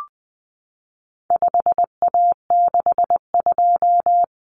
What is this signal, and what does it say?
Morse code, heard as a keyed tone, about 20 wpm, 700 Hz. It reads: 5A62
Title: Common Cut Numbers - 4-Digit 20wpm